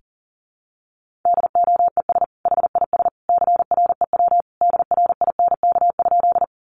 Morse code, heard as a keyed tone, about 40 wpm, 700 Hz. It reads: DOES HIS CREW DRINK?
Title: Sentences from Top 1500 Words 40wpm